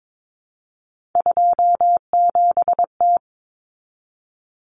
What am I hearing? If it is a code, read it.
Morse code, heard as a keyed tone, about 22 wpm, 700 Hz. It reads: 27T
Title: Common Cut Numbers - 3-Digit 22wpm